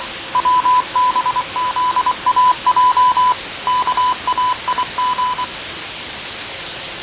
Morse code, heard as a keyed tone, about 24 wpm, 1000 Hz. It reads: WBZAJ KAIG
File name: fav22-6824usb-20251029-1420z-voice-test-win10